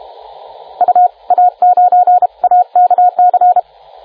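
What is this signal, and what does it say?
Morse code, heard as a keyed tone, about 32 wpm, spaced as for 28 wpm, 720 Hz, with radio noise. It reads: UA9AKC
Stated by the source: CW contest exchange in the 2014 Popov Cup VHF contest, RA9AX calling RA9BZ, recorded by the TR4W logger.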